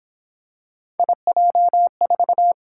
Morse code, heard as a keyed tone, about 27 wpm, 700 Hz. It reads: IJ4
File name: Speed-Racing - Three Letter-Number Combinations - 22wpm